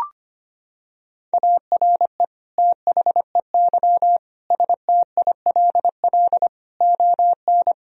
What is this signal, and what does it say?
Morse code, heard as a keyed tone, about 25 wpm, 700 Hz. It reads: ARE THEY STILL ON
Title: Sentences from Top 1000 Words - Review Part 3 - 25wpm